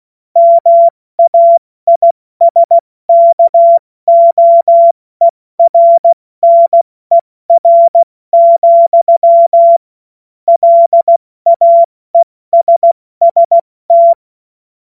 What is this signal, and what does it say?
Morse code, heard as a keyed tone, about 16 wpm, spaced as for 14 wpm, 680 Hz. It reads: MAISKOERNER, LAESST
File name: Witz-15-Juli-2025-70-F80